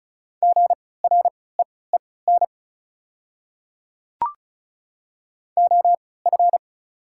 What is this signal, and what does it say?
Morse code, heard as a keyed tone, about 36 wpm, spaced as for 20 wpm, 700 Hz. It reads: GREEN OF
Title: 3x Character Spacing - Top 500 Words - 35wpm